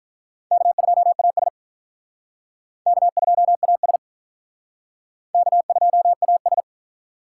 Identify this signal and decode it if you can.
Morse code, heard as a keyed tone, about 43 wpm, 700 Hz. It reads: K2AS K2AS K2AS
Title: Speed Racing - USA 1x2 Call Signs 35wpm